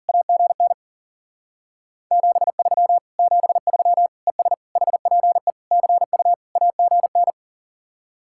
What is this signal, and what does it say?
Morse code, heard as a keyed tone, about 50 wpm, 700 Hz. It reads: AGN 73 73 ES HPE CU AGN
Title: Speed-Racing Common QSO Phrases Encore - 40wpm